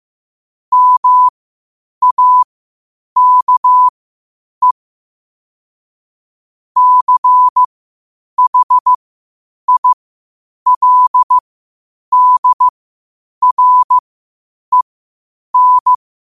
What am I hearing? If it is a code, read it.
Morse code, heard as a keyed tone, about 15 wpm, spaced as for 8 wpm, 1000 Hz. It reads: MAKE CHILDREN